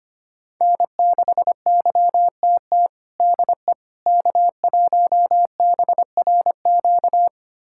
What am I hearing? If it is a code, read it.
Morse code, heard as a keyed tone, about 25 wpm, 700 Hz. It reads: N6YTT DE K1BRQ
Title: USA Callsign DE Callsign 25wpm